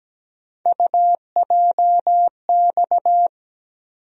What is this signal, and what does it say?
Morse code, heard as a keyed tone, about 17 wpm, 700 Hz. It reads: UJX
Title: USA Call Sign Postfixes - 3 Letters 17wpm